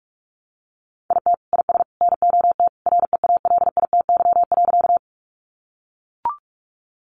Morse code, heard as a keyed tone, about 44 wpm, 700 Hz. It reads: IT IS NOT REALITY.
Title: Sentences from Top 1200 Words 45wpm